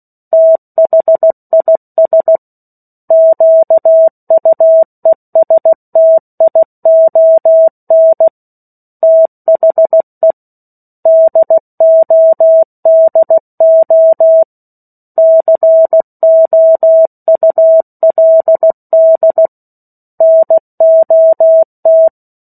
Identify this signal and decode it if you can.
Morse code, heard as a keyed tone, about 16 wpm, 650 Hz. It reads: THIS QUESTION THE DODO COULD NOT
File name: 03-Chapter-03 16 WPM